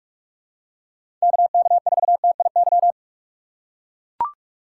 Morse code, heard as a keyed tone, about 44 wpm, 700 Hz. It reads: KK4TIY
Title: USA Call Signs - 6 Characters - 3x Character Spacing - 45wpm